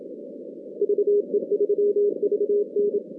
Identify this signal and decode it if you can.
Morse code, heard as a keyed tone, about 28 wpm, 410 Hz. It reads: VE3VN
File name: 2016 JAN CW NAQP / 2016 Jan naqp k5ot - 0-10